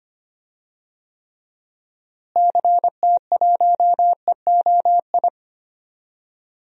Morse code, heard as a keyed tone, about 25 wpm, 700 Hz. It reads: CT1EOI
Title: DXCC-100 Call Signs - 6 Characters - Repeated 3x - 25wpm